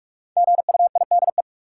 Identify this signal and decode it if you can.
Morse code, heard as a keyed tone, about 46 wpm, 700 Hz. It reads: GUIDE
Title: Sentences from Top 600 Words - New Word Drill - 45wpm